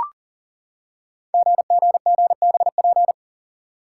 Morse code, heard as a keyed tone, about 40 wpm, 690 Hz. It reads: GGGBP
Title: Code Groups - 5 Letters - 40wpm